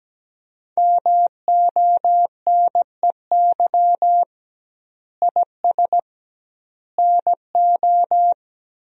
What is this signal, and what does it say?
Morse code, heard as a keyed tone, about 17 wpm, 700 Hz. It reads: MONEY IS NO
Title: 2x Word-Spacing - Sentences from Top 500 Words - 17wpm